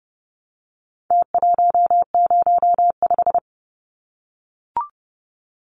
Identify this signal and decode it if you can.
Morse code, heard as a keyed tone, about 30 wpm, 700 Hz. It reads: T105
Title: Numbers - Contest Style - 4-Digit 30wpm